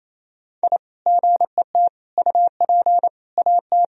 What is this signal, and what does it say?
Morse code, heard as a keyed tone, about 28 wpm, 700 Hz. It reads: I GET UP AT
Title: Sentences from Top 1000 Words - Review Part 2 - 28wpm